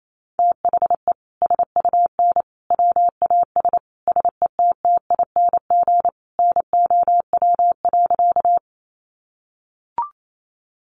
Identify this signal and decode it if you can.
Morse code, heard as a keyed tone, about 28 wpm, 700 Hz. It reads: THE SUN WAS SETTING NOW.
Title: Sentences from Top 1300 Words 28wpm